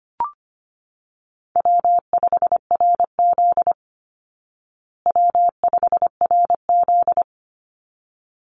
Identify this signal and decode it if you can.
Morse code, heard as a keyed tone, about 25 wpm, 700 Hz. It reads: W5RZ W5RZ
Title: Easy Call Signs Repeated 2x - 25wpm